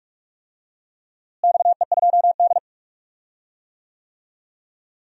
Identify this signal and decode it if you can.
Morse code, heard as a keyed tone, about 46 wpm, 700 Hz. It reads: XE2D
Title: DXCC-100 Call Signs - 4 Characters - Repeated 3x - 45wpm